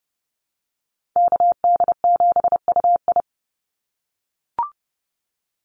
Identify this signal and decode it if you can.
Morse code, heard as a keyed tone, about 30 wpm, 700 Hz. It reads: KD7UI
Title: Call Signs - 30wpm